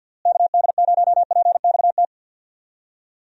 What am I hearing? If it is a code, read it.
Morse code, heard as a keyed tone, about 50 wpm, 700 Hz. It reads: KD0PXT
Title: USA Call Signs - 6 Characters - Repeated 2x - 50wpm